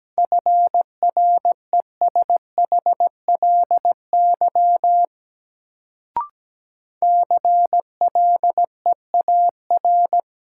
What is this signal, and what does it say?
Morse code, heard as a keyed tone, about 17 wpm, 700 Hz. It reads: FRESHLY CLEAR
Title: Sets of 5 Words Encore - New Word Drill - 17wpm